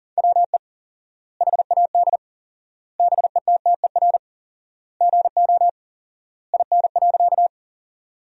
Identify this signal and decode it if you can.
Morse code, heard as a keyed tone, about 37 wpm, 700 Hz. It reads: WE HAD BETTER GO IN.
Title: 4x Word-Spacing - Sentences from Top 400 Words - 40wpm